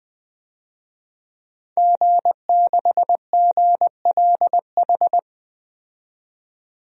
Morse code, heard as a keyed tone, about 21 wpm, 700 Hz. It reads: G6GLH
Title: DXCC-100 Call Signs - 5 Characters - 20wpm